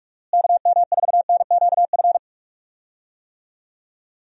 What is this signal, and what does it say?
Morse code, heard as a keyed tone, about 46 wpm, 700 Hz. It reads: KM4NQF
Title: USA Call Signs - 6 Characters - Repeated 3x - 45wpm